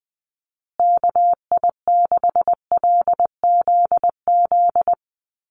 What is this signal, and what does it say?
Morse code, heard as a keyed tone, about 20 wpm, 700 Hz. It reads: KI6LZZ
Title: USA Call Signs Repeated 3x 20wpm